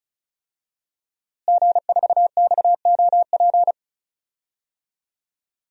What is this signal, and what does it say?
Morse code, heard as a keyed tone, about 34 wpm, 700 Hz. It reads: G4XOP
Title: DXCC-100 Call Signs - 5 Characters - Repeated 2x - 35wpm